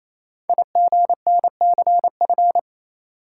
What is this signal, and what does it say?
Morse code, heard as a keyed tone, about 28 wpm, 700 Hz. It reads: IGNCF
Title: Code Groups - 5 Letters - 28wpm